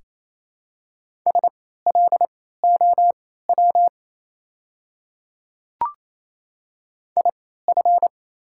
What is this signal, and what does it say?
Morse code, heard as a keyed tone, about 28 wpm, spaced as for 16 wpm, 700 Hz. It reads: SLOW IF
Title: 3x Character Spacing - Top 500 Words - 28wpm